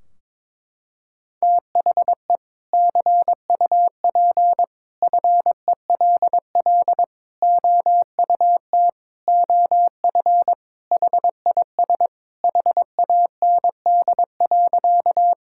Morse code, heard as a keyed tone, about 22 wpm, 700 Hz. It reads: THE CUP FELL OUT OF HIS HAND.
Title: Sentences from Top 1600 Words 22wpm